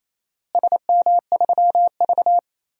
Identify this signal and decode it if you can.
Morse code, heard as a keyed tone, about 28 wpm, 700 Hz. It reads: SM3V
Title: DXCC-100 Call Signs - 4 Characters - Repeated 2x - 28wpm